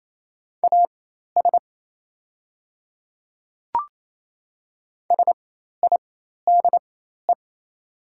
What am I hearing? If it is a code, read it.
Morse code, heard as a keyed tone, about 27 wpm, spaced as for 13 wpm, 690 Hz. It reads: AS SIDE
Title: 4x Character Spacing - Top 300 Words - 28wpm